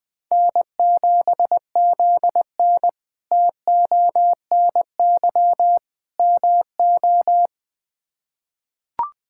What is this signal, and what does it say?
Morse code, heard as a keyed tone, about 21 wpm, 700 Hz. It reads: N7ZN TONY MO
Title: K1USN SST QSO Practice - 20wpm